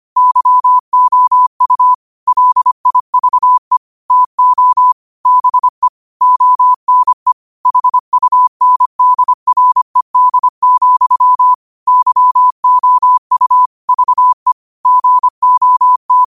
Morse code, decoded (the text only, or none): YOU LIVE TO BE ONE HUNDRED, YOU VE GOT